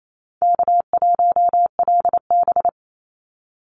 Morse code, heard as a keyed tone, about 28 wpm, 700 Hz. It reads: K1LB
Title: NA SOTA Chaser Callsigns 28wpm